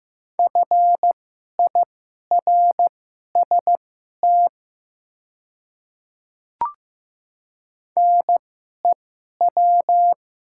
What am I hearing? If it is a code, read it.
Morse code, heard as a keyed tone, about 15 wpm, spaced as for 11 wpm, 700 Hz. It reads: FIRST NEW
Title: 2x Character Spacing - Top 500 Words - 15wpm